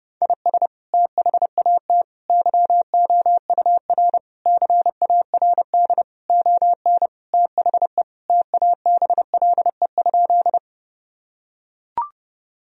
Morse code, heard as a keyed tone, about 32 wpm, 700 Hz. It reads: IS THAT YOUR CARD ON THE TABLE?
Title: Sentences from Top 1000 Words - Review Part 4 - 30wpm